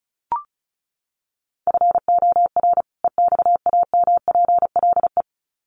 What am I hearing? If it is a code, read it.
Morse code, heard as a keyed tone, about 34 wpm, 700 Hz. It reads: FOR EXAMPLE
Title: Sets of 2 Words 35wpm